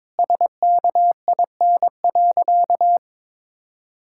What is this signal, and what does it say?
Morse code, heard as a keyed tone, about 21 wpm, 700 Hz. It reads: SKIN.
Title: Sentences from Top 1600 Words 22wpm